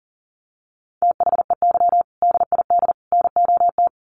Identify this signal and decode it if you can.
Morse code, heard as a keyed tone, about 40 wpm, 700 Hz. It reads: THEY DID NOT
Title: Sets of 3 Words 40wpm